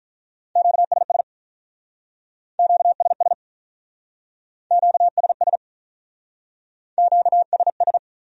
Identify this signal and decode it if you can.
Morse code, heard as a keyed tone, about 44 wpm, 700 Hz. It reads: QSS QSS QSS QSS